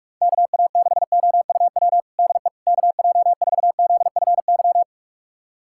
Morse code, heard as a keyed tone, about 46 wpm, 700 Hz. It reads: KA6OUW DE KJ4ZFY
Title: USA Callsign DE Callsign - 6 Characters - 45wpm